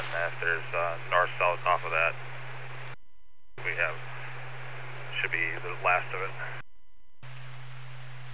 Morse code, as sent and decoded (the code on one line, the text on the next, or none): none